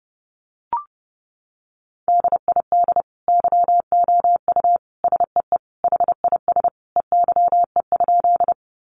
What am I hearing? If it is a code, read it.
Morse code, heard as a keyed tone, about 30 wpm, 700 Hz. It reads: DID YOU SEE HIS EYE?
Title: Sentences from Top 300 Words 30wpm